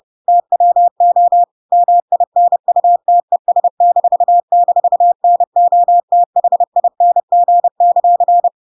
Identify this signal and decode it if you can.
Morse code, heard as a keyed tone, about 30 wpm, 700 Hz. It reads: TWO MINUTES--NOTHING;
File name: Pinocchio / Pinocchio ch1 30 30 ik1hge